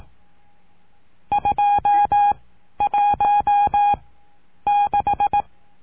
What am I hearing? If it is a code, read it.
Morse code, heard as a keyed tone, about 18 wpm, spaced as for 10 wpm, 820 Hz. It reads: 216
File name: cursos / CURSO DE TELEGRAFIA / Telegrafia - Trilha 19